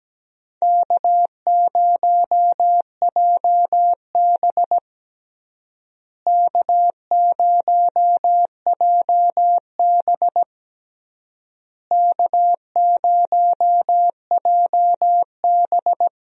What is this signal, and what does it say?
Morse code, heard as a keyed tone, about 17 wpm, 700 Hz. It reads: K0JB K0JB K0JB
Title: USA Call Signs - 4 Characters - Repeated 3x - 17wpm